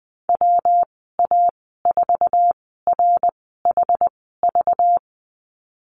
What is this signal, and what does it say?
Morse code, heard as a keyed tone, about 20 wpm, spaced as for 15 wpm, 700 Hz. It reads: WA4RHV
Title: 213 Lesson 34g - Number 8 - Call Signs 20wpm